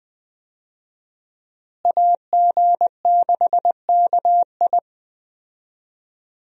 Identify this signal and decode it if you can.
Morse code, heard as a keyed tone, about 20 wpm, 700 Hz. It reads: AG6KI
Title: USA Call Signs - 5 Characters - Repeated 3x - 20wpm